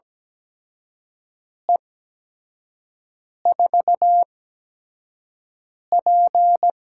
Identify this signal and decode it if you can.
Morse code, heard as a keyed tone, about 17 wpm, spaced as for 5 wpm, 700 Hz. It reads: E4P
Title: USA Call Signs - 4 Characters - 8x Character Spacing - 17wpm